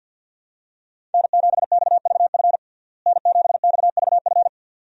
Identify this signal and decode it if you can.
Morse code, heard as a keyed tone, about 52 wpm, 700 Hz. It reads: N7XVF N7XVF
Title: USA Call Signs - 5 Characters - Repeated 2x - 50wpm